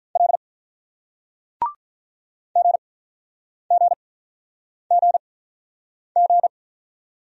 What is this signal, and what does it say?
Morse code, heard as a keyed tone, about 48 wpm, spaced as for 9 wpm, 700 Hz. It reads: L GGGG